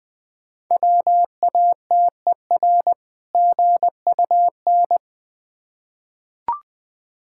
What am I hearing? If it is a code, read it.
Morse code, heard as a keyed tone, about 20 wpm, 690 Hz. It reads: WATER GUN
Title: Difficult Words with Hints - 20wpm